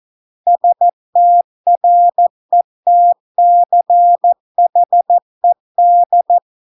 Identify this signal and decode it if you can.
Morse code, heard as a keyed tone, about 14 wpm, 700 Hz. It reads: STRETCHED